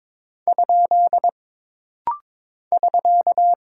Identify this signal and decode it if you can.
Morse code, heard as a keyed tone, about 22 wpm, 700 Hz. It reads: ? <SK>
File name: practice / 22wpm / Single Punctuation-Prosign - Rapid-Fire - 22wpm